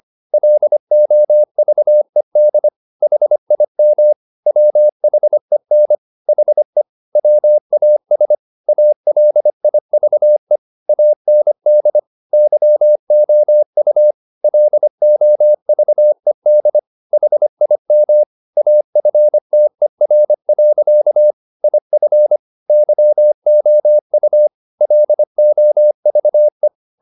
Morse code, heard as a keyed tone, about 25 wpm, 600 Hz. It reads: LOVED HIM WHEN HE WAS ALIVE AND YOU LOVED HIM AFTER. IF YOU LOVE